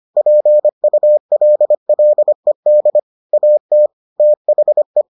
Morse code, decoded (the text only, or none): PULLED AT THE